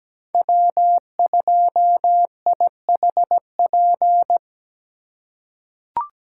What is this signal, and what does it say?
Morse code, heard as a keyed tone, about 17 wpm, 700 Hz. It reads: W2IHP
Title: USA Call Signs - 5 Characters - 2x Character Spacing - 17wpm